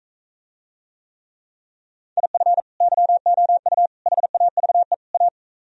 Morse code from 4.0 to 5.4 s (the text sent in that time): HAVE A